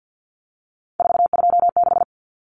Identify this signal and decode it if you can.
Morse code, heard as a keyed tone, about 50 wpm, 700 Hz. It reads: 426
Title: Common Cut Numbers - 3-Digit 50wpm